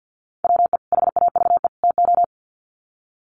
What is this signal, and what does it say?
Morse code, heard as a keyed tone, about 50 wpm, 700 Hz. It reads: WE HAVE TO
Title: Sets of 3 Words 50wpm